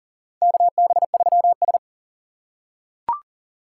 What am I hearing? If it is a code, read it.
Morse code, heard as a keyed tone, about 38 wpm, 700 Hz. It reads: KB3S
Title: USA Call Signs - 4 Characters - 4x Character Spacing - 40wpm